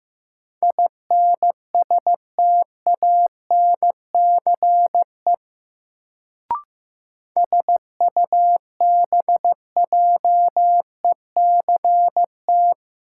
Morse code, heard as a keyed tone, about 15 wpm, 700 Hz. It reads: INSTANCE SUBJECT
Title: Sentences from Top 800 Words - New Word Drill - 15wpm